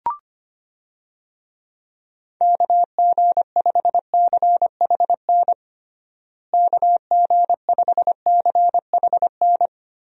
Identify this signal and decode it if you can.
Morse code, heard as a keyed tone, about 25 wpm, 700 Hz. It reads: KG5CHN KG5CHN
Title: USA Call Signs - 6 Characters - Repeated 3x - 25wpm